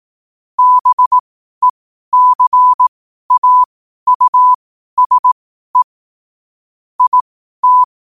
Morse code, heard as a keyed tone, about 18 wpm, spaced as for 12 wpm, 1000 Hz. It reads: BECAUSE IT
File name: qotdfiles / 12wpm